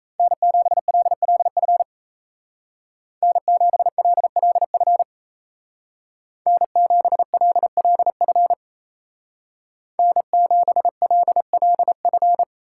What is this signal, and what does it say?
Morse code, heard as a keyed tone, about 34 wpm, 700 Hz. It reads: N7LLF N7LLF N7LLF N7LLF
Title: Speed Racing - USA 1x3 Call Signs 28wpm